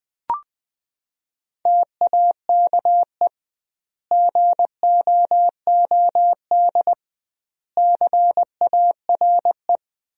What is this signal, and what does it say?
Morse code, heard as a keyed tone, about 21 wpm, 700 Hz. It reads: TAKE GOOD CARE
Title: 2x Word-Spacing - Sentences from Top 400 Words - 20wpm